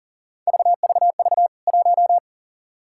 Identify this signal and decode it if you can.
Morse code, heard as a keyed tone, about 40 wpm, 700 Hz. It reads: VVV 1
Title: K1USN SST QSO Practice 40wpm